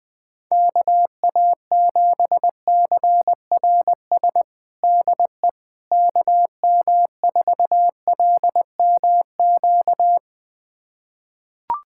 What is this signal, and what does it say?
Morse code, heard as a keyed tone, about 20 wpm, 700 Hz. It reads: KA7CRS DE KM4LMQ
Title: USA Callsign DE Callsign 20wpm